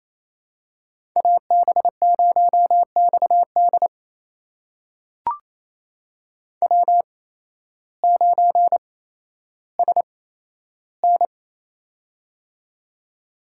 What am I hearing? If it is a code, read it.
Morse code, heard as a keyed tone, about 28 wpm, spaced as for 13 wpm, 700 Hz. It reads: AB0XD W9SN
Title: USA Call Signs - 8x Character Spacing - 28wpm